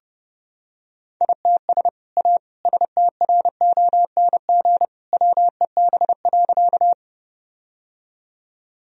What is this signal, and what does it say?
Morse code, heard as a keyed tone, about 30 wpm, 700 Hz. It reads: ITS A STRONG WEB.
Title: Sentences from Top 1000 Words - Review Part 4 - 30wpm